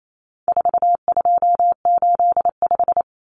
Morse code, heard as a keyed tone, about 28 wpm, 700 Hz. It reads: 4285
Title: Common Cut Numbers - 4-Digit 28wpm